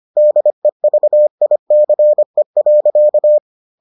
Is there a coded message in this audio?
Yes: Morse code from 0.2 to 3.4 s, heard as a keyed tone, about 25 wpm, 600 Hz. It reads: DEVICE.